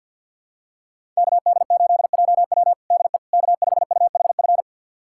Morse code, heard as a keyed tone, about 49 wpm, 700 Hz. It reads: KD8JW DE K5UHF
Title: USA Callsign DE Callsign - 5 Characters - 50wpm